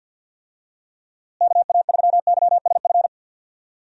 Morse code, heard as a keyed tone, about 53 wpm, 700 Hz. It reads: KA3YSF
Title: USA Call Signs - 6 Characters - Repeated 3x - 50wpm